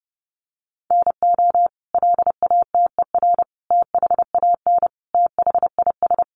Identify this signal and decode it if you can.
Morse code, heard as a keyed tone, about 28 wpm, 700 Hz. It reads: NO LATER THAN THIS
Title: Sentences from Top 700 Words 30wpm